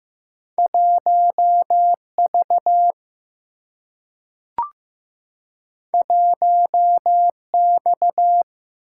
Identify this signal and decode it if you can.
Morse code, heard as a keyed tone, about 15 wpm, 700 Hz. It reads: 1V 1X
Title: Single-Letter Single-Number 15wpm